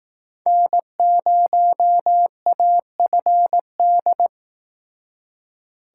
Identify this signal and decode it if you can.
Morse code, heard as a keyed tone, about 18 wpm, 700 Hz. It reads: N0AFD
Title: Speed Racing - USA 1x3 Call Signs 15wpm